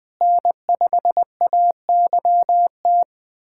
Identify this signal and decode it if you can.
Morse code, heard as a keyed tone, about 20 wpm, 700 Hz. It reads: N5AYT
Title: USA Callsign DE Callsign - 5 Characters - 20wpm